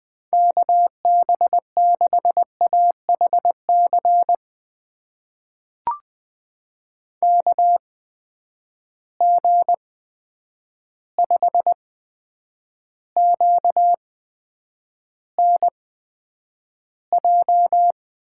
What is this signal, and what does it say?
Morse code, heard as a keyed tone, about 20 wpm, spaced as for 9 wpm, 700 Hz. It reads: KB6AHC KG5QNJ
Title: USA Call Signs - 6 Characters - 8x Character Spacing - 20wpm